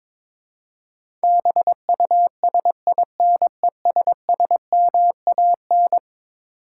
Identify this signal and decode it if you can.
Morse code, heard as a keyed tone, about 22 wpm, 700 Hz. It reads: BUSINESSMAN
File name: practice / 22wpm / Eleven Letter Words - 22wpm